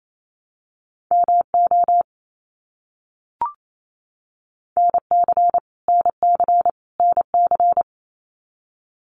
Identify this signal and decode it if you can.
Morse code, heard as a keyed tone, about 28 wpm, 700 Hz. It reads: MO NC NC NC
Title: US States Repeated 3x 28wpm